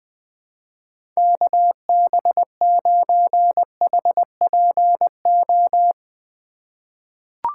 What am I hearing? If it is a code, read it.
Morse code, heard as a keyed tone, about 20 wpm, 700 Hz. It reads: KB9HPO
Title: USA Call Signs - 6 Characters - 2x Character Spacing - 20wpm